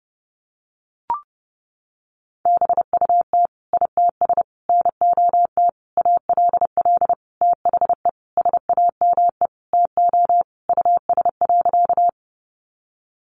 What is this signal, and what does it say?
Morse code, heard as a keyed tone, about 30 wpm, 700 Hz. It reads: BUT ITS NOT ALL THE SAME TO US.
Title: Sentences from Top 600 Words 30wpm